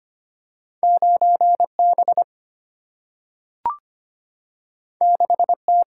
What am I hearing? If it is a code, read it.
Morse code, heard as a keyed tone, about 25 wpm, 700 Hz. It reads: 9 B 6 T
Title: Single-Letter Single-Number 25wpm